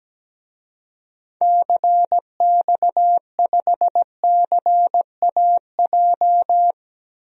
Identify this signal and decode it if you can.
Morse code, heard as a keyed tone, about 17 wpm, 700 Hz. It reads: CX5CAJ
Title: DXCC-100 Call Signs - 6 Characters - Repeated 2x - 17wpm